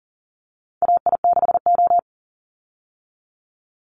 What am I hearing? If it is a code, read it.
Morse code, heard as a keyed tone, about 40 wpm, 700 Hz. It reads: AI6O
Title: Call Signs Repeated 3x - 40wpm